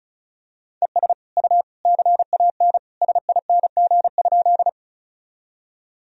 Morse code, heard as a keyed tone, about 36 wpm, 700 Hz. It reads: ES U CAN SING?
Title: Sentences from Top 400 Words and Abbreviations 35wpm